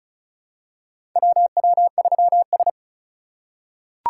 Morse code, heard as a keyed tone, about 34 wpm, 700 Hz. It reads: WW3S
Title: USA Call Signs - 4 Characters - 4x Character Spacing - 35wpm